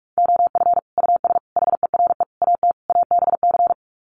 Morse code, heard as a keyed tone, about 44 wpm, 700 Hz. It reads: OF US HERE AT ABC
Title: Sets of 5 Words 45wpm